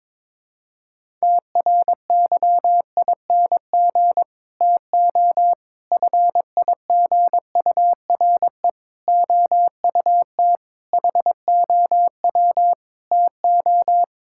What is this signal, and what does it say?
Morse code, heard as a keyed tone, about 22 wpm, 700 Hz. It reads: TRYING TO FIGURE OUT HOW TO
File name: practice / 22wpm / Sets of 6 Words - 22wpm